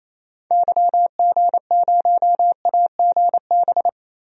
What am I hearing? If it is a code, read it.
Morse code, heard as a keyed tone, about 28 wpm, 700 Hz. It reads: YG0AGB
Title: DXCC-100 Call Signs - 6 Characters - 28wpm